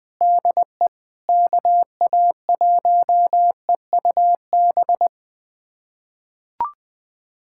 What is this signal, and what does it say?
Morse code, heard as a keyed tone, about 20 wpm, 700 Hz. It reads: DE KA1EUB
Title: USA Callsign DE Callsign 20wpm